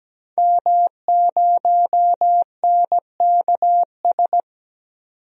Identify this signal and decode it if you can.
Morse code, heard as a keyed tone, about 17 wpm, 700 Hz. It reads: M0NKS
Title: DXCC-100 Call Signs - 5 Characters - 17wpm